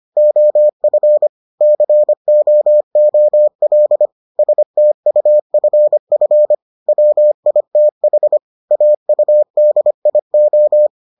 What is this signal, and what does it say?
Morse code, heard as a keyed tone, about 25 wpm, 600 Hz. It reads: OF COOL STUFF WITH AUDIO